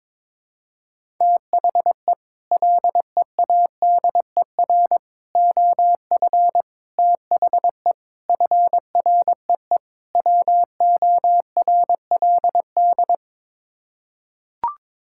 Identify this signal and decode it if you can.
Morse code, heard as a keyed tone, about 21 wpm, 700 Hz. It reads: THE LEADER OF THE FREE WORLD
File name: practice / 22wpm / Sets of 6 Words - 22wpm